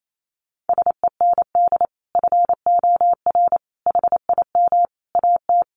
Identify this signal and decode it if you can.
Morse code, heard as a keyed tone, about 28 wpm, 700 Hz. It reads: SEND FOR HIM AT
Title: Sentences from Top 1500 Words 28wpm